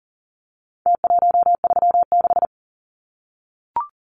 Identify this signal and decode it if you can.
Morse code, heard as a keyed tone, about 38 wpm, 700 Hz. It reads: T136
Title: Numbers - Contest Style - 4-Digit 40wpm